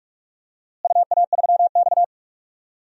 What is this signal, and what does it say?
Morse code, heard as a keyed tone, about 41 wpm, 700 Hz. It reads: UA3X
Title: DXCC-100 Call Signs - 4 Characters - Repeated 2x - 45wpm